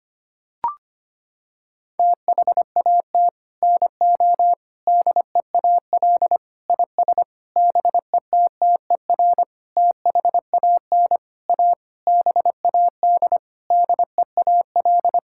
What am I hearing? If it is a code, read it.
Morse code, heard as a keyed tone, about 25 wpm, 700 Hz. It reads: THAT NO DEAL IS BETTER THAN A BAD DEAL
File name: Sets of 9 Words - 25wpm